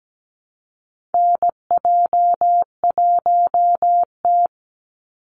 Morse code, heard as a keyed tone, about 17 wpm, 700 Hz. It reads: NJ1T
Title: Call Signs Repeated 3x - 17wpm